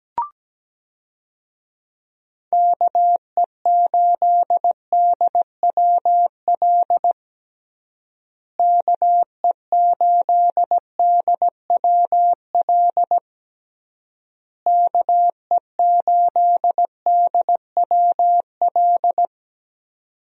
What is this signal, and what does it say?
Morse code, heard as a keyed tone, about 17 wpm, 700 Hz. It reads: KE8DWL KE8DWL KE8DWL
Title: USA Call Signs - 6 Characters - Repeated 3x - 17wpm